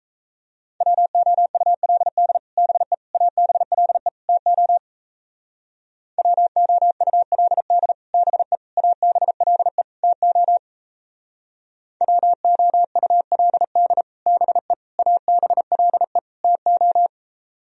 Morse code, heard as a keyed tone, about 36 wpm, 700 Hz. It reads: WOULD BE ABLE TO WOULD BE ABLE TO WOULD BE ABLE TO